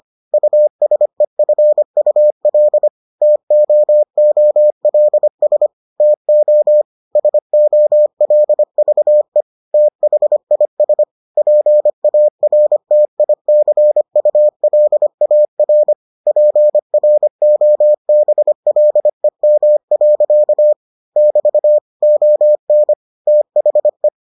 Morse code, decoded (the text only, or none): USEFUL TOOLS TO SOLVE THIS PARTICULAR PROBLEM. <BT> ON THE